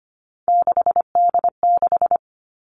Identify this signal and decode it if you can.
Morse code, heard as a keyed tone, about 25 wpm, 700 Hz. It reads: 6D6
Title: Three Challenging Letter-Number Combinations 25wpm